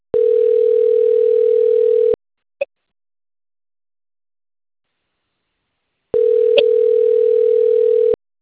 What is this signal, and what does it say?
Ringback tone (North America), 440 + 480 Hz.